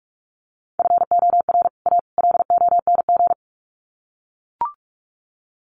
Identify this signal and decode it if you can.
Morse code, heard as a keyed tone, about 44 wpm, 700 Hz. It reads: FOR A LONG
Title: Sets of 3 Words 45wpm